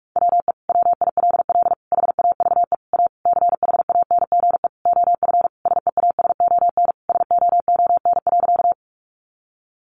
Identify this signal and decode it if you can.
Morse code, heard as a keyed tone, about 44 wpm, 700 Hz. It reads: WE WILL HAVE A CHANGE OF SEASON SOON.
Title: Sentences from Top 1400 Words 45wpm